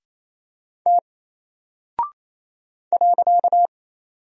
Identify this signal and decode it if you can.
Morse code, heard as a keyed tone, about 28 wpm, 700 Hz. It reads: T .
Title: Single Character (Letter:Number:Punctuation:Prosign) Alt 28wpm